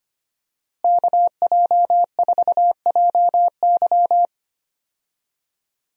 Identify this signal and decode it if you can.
Morse code, heard as a keyed tone, about 25 wpm, 700 Hz. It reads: KJ4JY
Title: USA Call Signs - 5 Characters - Repeated 2x - 25wpm